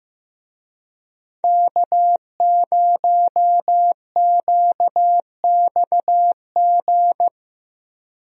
Morse code, heard as a keyed tone, about 15 wpm, 700 Hz. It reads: K0QXG
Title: USA Call Signs - 5 Characters - 15wpm